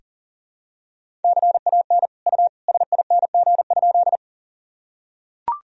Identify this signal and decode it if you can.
Morse code, heard as a keyed tone, about 40 wpm, 700 Hz. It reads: CAN U SING?
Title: Sentences from Top 400 Words and Abbreviations 40wpm